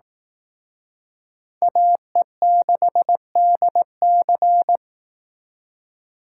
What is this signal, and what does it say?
Morse code, heard as a keyed tone, about 18 wpm, 700 Hz. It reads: AE6DC
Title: Speed Racing - USA 2x2 Call Signs 15wpm